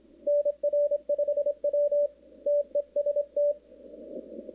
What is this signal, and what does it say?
Morse code, heard as a keyed tone, about 26 wpm, 580 Hz. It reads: NR5W TEST
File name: contests / 2019-cw-wpx-kt5j-k5pi- -2330 0-11